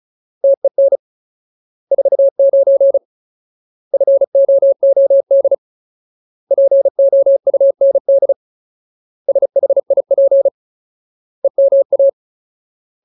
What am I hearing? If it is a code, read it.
Morse code, heard as a keyed tone, about 34 wpm, 550 Hz. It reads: TEN 49 FOOD POUND SHIP EMA
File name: Ultimate Word List / 35WPM Ultimate